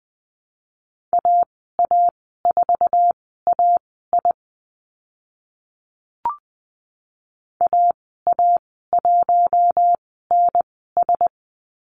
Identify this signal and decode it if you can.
Morse code, heard as a keyed tone, about 20 wpm, spaced as for 14 wpm, 700 Hz. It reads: AA4AI AA1NS
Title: 076 Lesson 13g - Letter L - Call Signs 20wpm